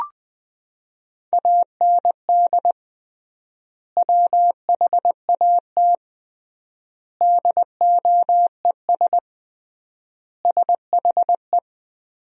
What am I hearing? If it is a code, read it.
Morse code, heard as a keyed tone, about 20 wpm, 700 Hz. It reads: AND WHAT DOES SHE
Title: 3x Word-Spacing - Sentences from Top 400 Words - 20wpm